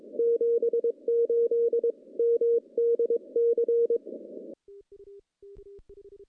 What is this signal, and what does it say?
Morse code, heard as a keyed tone, about 22 wpm, 470 Hz. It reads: 78 MDC
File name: contests / CW-SS-k5tr- -0100